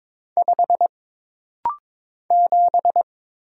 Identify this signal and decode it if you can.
Morse code, heard as a keyed tone, about 22 wpm, 700 Hz. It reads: 5 7
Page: This rapid-fire timing is geared to push your ICR (Instant Character Recognition) skills.